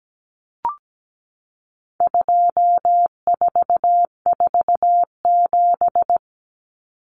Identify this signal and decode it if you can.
Morse code, heard as a keyed tone, about 17 wpm, 700 Hz. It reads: 2447
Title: Numbers - 4-Digit 17wpm